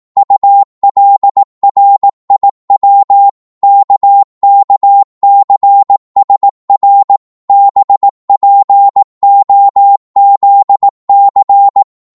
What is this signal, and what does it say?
Morse code, heard as a keyed tone, about 18 wpm, 800 Hz. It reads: ULRIW KKCSR BPOZC
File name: Morse / ltr-18wpm